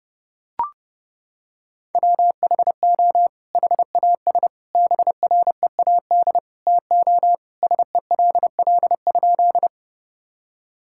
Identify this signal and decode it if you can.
Morse code, heard as a keyed tone, about 32 wpm, 700 Hz. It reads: WHO HAS BREAD TO SELL?
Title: Sentences from Top 1600 Words 30wpm